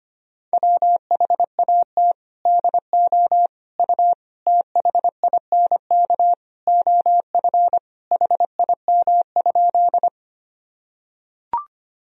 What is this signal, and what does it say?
Morse code, heard as a keyed tone, about 25 wpm, 700 Hz. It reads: WHAT DO U THINK OF HIM?
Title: Sentences from Top 200 Words and Abbreviations 25wpm